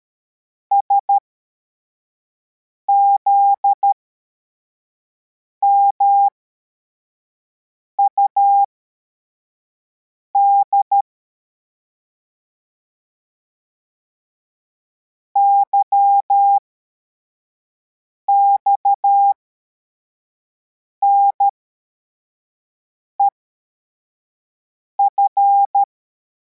Morse code, heard as a keyed tone, about 13 wpm, spaced as for 4 wpm, 800 Hz. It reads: SZMUD YXNEF
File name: formation / cw / Text62